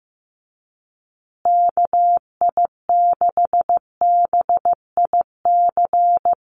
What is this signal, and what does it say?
Morse code, heard as a keyed tone, about 15 wpm, 700 Hz. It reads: KI6BIC
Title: USA Call Signs Repeated 3x 15wpm